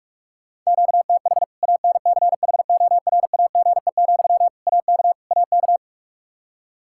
Morse code, heard as a keyed tone, about 46 wpm, 700 Hz. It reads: QTH ANCHORAGE, AK AK
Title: K1USN SST QSO Practice 45wpm